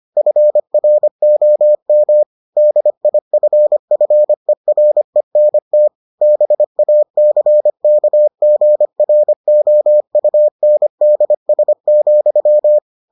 Morse code, heard as a keyed tone, about 25 wpm, 600 Hz. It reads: FROM DIFFERENT BACKGROUNDS,